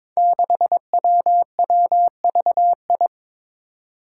Morse code, heard as a keyed tone, about 22 wpm, 700 Hz. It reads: 6WWVI
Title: Five Challenging Letter-Number Combinations 22wpm